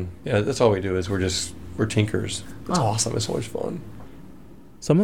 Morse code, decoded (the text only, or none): none